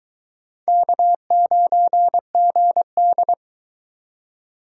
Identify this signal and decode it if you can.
Morse code, heard as a keyed tone, about 22 wpm, 700 Hz. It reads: K9GD